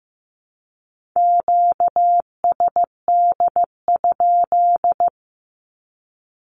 Morse code, heard as a keyed tone, about 15 wpm, 700 Hz. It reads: QSD?
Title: Q-Codes with Spoken Meaning - 15wpm